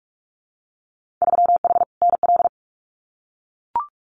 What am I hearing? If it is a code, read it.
Morse code, heard as a keyed tone, about 44 wpm, 700 Hz. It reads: 3H NL
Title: Winter Field Day Exchanges Encore 45wpm